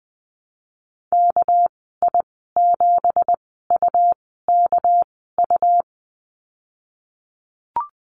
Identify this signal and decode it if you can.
Morse code, heard as a keyed tone, about 20 wpm, spaced as for 15 wpm, 700 Hz. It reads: KI7UKU
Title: 231 Lesson 37g - Letter x - Call Signs 20wpm